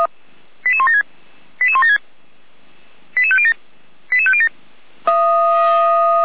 Five-tone selective call (ZVEI), heard as quick runs of five tones, tones of about 70 ms. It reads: 88176 88176 88507 88507